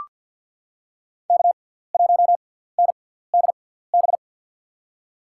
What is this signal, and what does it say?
Morse code, heard as a keyed tone, about 45 wpm, spaced as for 17 wpm, 700 Hz. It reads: K1NDB
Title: USA Call Signs - 6x Character Spacing - 50wpm